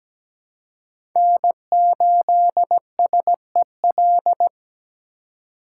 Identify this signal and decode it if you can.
Morse code, heard as a keyed tone, about 17 wpm, 700 Hz. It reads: N8SEL
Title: USA Call Signs - 5 Characters - 3x Character Spacing - 17wpm